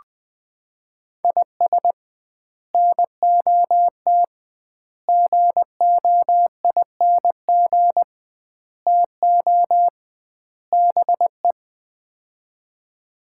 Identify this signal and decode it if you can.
Morse code, heard as a keyed tone, about 20 wpm, 700 Hz. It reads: IS NOT GOING TO BE
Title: 2x Word-Spacing - Sets of 5 Words - 20wpm